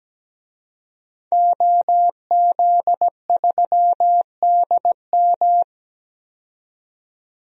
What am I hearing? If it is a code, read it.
Morse code, heard as a keyed tone, about 17 wpm, 700 Hz. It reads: OZ3DM
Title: DXCC-100 Call Signs - 5 Characters - 17wpm